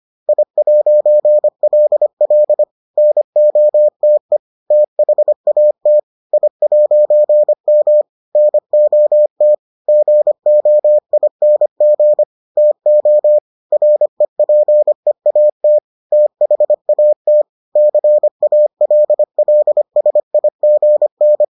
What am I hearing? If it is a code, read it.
Morse code, heard as a keyed tone, about 25 wpm, 600 Hz. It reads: I'LL NOTE THAT I'M NOT GOING TO REPEAT THAT CALLSIGN